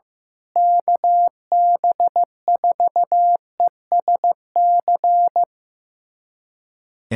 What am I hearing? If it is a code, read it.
Morse code, heard as a keyed tone, about 15 wpm, 700 Hz. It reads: KB4ESC
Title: USA Callsign DE Callsign 15wpm